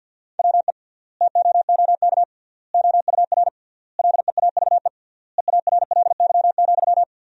Morse code, heard as a keyed tone, about 50 wpm, 700 Hz. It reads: WE TOOK OUR LEAVE EARLY,
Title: 3x Word-Spacing - Sentences from Top 400 Words - 50wpm